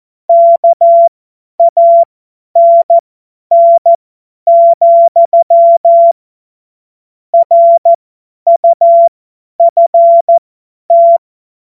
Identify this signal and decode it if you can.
Morse code, heard as a keyed tone, about 14 wpm, spaced as for 10 wpm, 680 Hz. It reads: KANN, RUFT
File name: Morsekurs / Witze / Witz 15. August 2023 50 F70